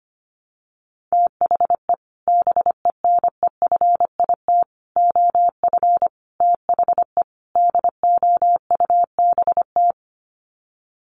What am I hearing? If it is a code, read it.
Morse code, heard as a keyed tone, about 25 wpm, 690 Hz. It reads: THE BENEFIT OF THE DOUBT
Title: Sets of 5 Words Encore 25wpm